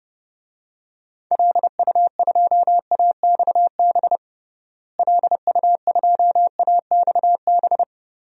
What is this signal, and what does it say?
Morse code, heard as a keyed tone, about 30 wpm, 700 Hz. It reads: LU2AXB LU2AXB
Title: DXCC-100 Call Signs - 6 Characters - Repeated 3x - 30wpm